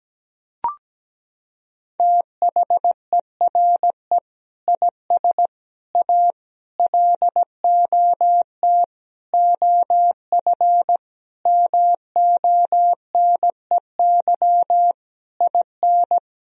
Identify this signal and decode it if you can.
Morse code, heard as a keyed tone, about 17 wpm, 700 Hz. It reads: THERE IS A LOT OF MONEY IN
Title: Sentences from Top 1000 Words - Review Part 4 - 17wpm